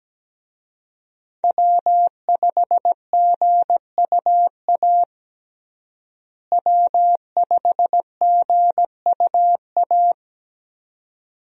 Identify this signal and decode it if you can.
Morse code, heard as a keyed tone, about 17 wpm, 700 Hz. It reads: W5GUA W5GUA
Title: USA Call Signs - 5 Characters - Repeated 2x - 17wpm